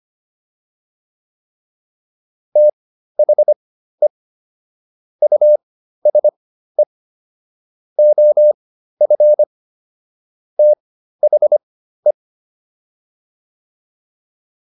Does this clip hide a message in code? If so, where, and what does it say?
Yes: Morse code from 2.6 to 12.1 s, heard as a keyed tone, about 25 wpm, spaced as for 13 wpm, 600 Hz. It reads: THE USE OF THE